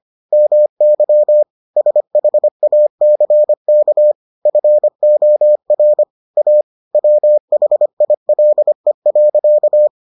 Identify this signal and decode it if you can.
Morse code, heard as a keyed tone, about 25 wpm, 600 Hz. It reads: MY SHACK FOR A WHILE.